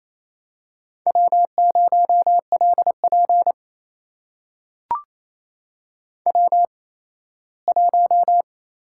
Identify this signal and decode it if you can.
Morse code, heard as a keyed tone, about 28 wpm, spaced as for 17 wpm, 700 Hz. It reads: W0LP W1
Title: USA Call Signs - 4 Characters - 8x Character Spacing - 28wpm